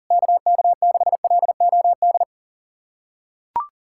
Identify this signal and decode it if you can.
Morse code, heard as a keyed tone, about 37 wpm, 700 Hz. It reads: KK6LOD
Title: USA Callsign DE Callsign - 6 Characters - 40wpm